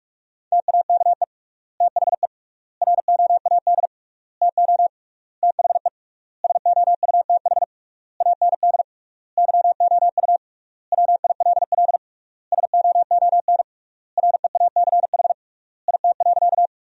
Morse code, heard as a keyed tone, about 46 wpm, 700 Hz. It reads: TAKE THE ROAD TO THE SOUTH AND YOU WILL SOON REACH IT.
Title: 3x Word-Spacing - Sentences from Top 400 Words - 45wpm